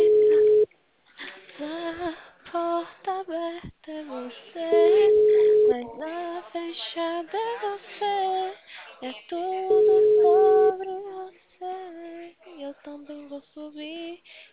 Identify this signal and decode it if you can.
Ringback tone (Europe, 425 Hz).
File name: out- -2013- - - .26831